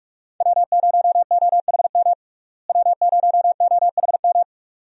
Morse code, heard as a keyed tone, about 46 wpm, 700 Hz. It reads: W0OHM W0OHM
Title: USA Call Signs - 5 Characters - Repeated 2x - 45wpm